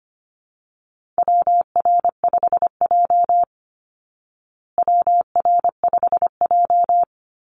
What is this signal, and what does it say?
Morse code, heard as a keyed tone, about 25 wpm, 700 Hz. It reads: WR5J WR5J
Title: Call Signs Repeated 2x - 25wpm